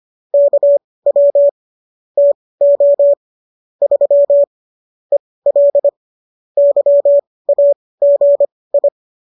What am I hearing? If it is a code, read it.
Morse code, heard as a keyed tone, about 25 wpm, spaced as for 18 wpm, 570 Hz. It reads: KW TO 3 EL YAGI